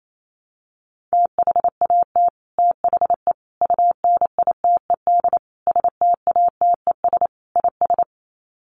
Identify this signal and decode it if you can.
Morse code, heard as a keyed tone, about 28 wpm, 700 Hz. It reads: THAT THE UNITED STATES IS
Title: Sets of 5 Words Encore 28wpm